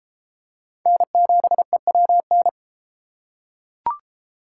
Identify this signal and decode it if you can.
Morse code, heard as a keyed tone, about 32 wpm, 700 Hz. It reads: N7EWN